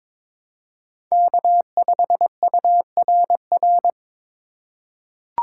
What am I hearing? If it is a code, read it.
Morse code, heard as a keyed tone, about 23 wpm, 700 Hz. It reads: K5URR
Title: USA Call Signs - 5 Characters - 4x Character Spacing - 22wpm